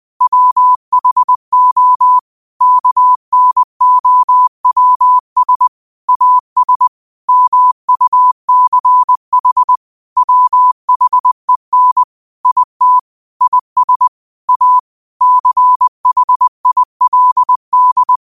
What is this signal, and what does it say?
Morse code, heard as a keyed tone, about 20 wpm, 1000 Hz. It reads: WHO KNOWS AS MUCH WHEN IT IS A CHILD
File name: qotdfiles / 20wpm